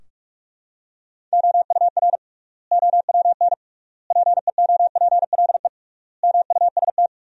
Morse code, heard as a keyed tone, about 46 wpm, 700 Hz. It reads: OUR OWN PEOPLE MUST
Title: 3x Word-Spacing - Sentences from Top 300 Words - 45wpm